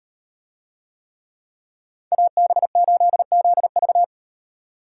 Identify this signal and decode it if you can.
Morse code, heard as a keyed tone, about 37 wpm, 700 Hz. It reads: AB8ZV